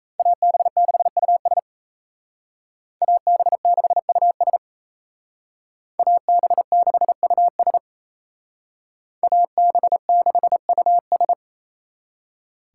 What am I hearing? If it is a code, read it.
Morse code, heard as a keyed tone, about 37 wpm, 700 Hz. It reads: AB6US AB6US AB6US AB6US